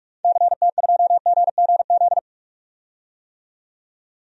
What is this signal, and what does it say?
Morse code, heard as a keyed tone, about 47 wpm, 700 Hz. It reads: CT2GGZ
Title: DXCC-100 Call Signs - 6 Characters - 45wpm